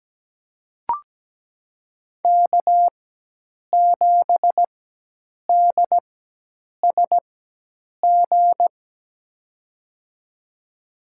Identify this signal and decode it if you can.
Morse code, heard as a keyed tone, about 17 wpm, spaced as for 8 wpm, 700 Hz. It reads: K7DSG
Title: USA Call Signs - 5 Characters - 4x Character Spacing - 17wpm